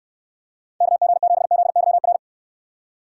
Morse code, heard as a keyed tone, about 69 wpm, 700 Hz. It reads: KG7ZYR